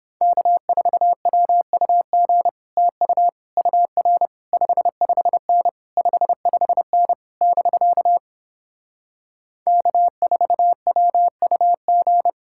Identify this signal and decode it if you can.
Morse code, heard as a keyed tone, about 31 wpm, 700 Hz. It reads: K4WUG TU UR 55N 55N <BK> K4WUG